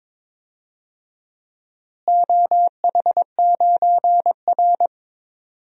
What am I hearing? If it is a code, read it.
Morse code, heard as a keyed tone, about 22 wpm, 700 Hz. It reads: OH9R
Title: DXCC-100 Call Signs - 4 Characters - Repeated 3x - 22wpm